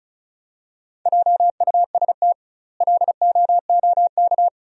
Morse code, heard as a keyed tone, about 34 wpm, 700 Hz. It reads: JUST LOOK
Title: 2x Word-Spacing - Sentences from Top 200 Words - 35wpm